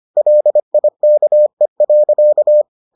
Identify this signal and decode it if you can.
Morse code, heard as a keyed tone, about 25 wpm, 600 Hz. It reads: LIKE.